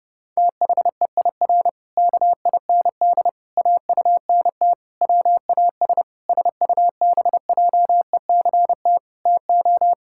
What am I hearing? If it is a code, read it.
Morse code, heard as a keyed tone, about 30 wpm, 700 Hz. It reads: THEIR KIND AUNT WAS SUBJECT TO
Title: Pangram Challenge v2 - 30wpm